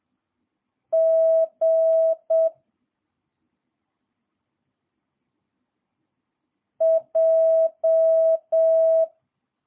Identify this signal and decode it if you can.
Morse code, heard as a keyed tone, about 7 wpm, 650 Hz. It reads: G J
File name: jh8cbh / 2022cw